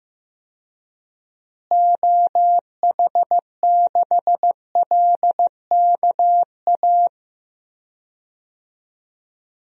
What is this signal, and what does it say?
Morse code, heard as a keyed tone, about 15 wpm, 700 Hz. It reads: OH6LKA
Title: DXCC-100 Call Signs - 6 Characters - Repeated 3x - 15wpm